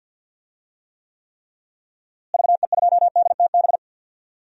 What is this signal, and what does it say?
Morse code, heard as a keyed tone, about 47 wpm, 700 Hz. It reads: VE2DTB
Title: DXCC-100 Call Signs - 6 Characters - Repeated 3x - 50wpm